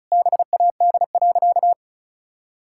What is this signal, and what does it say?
Morse code, heard as a keyed tone, about 36 wpm, 700 Hz. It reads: BAD.
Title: Sentences from Top 1600 Words 35wpm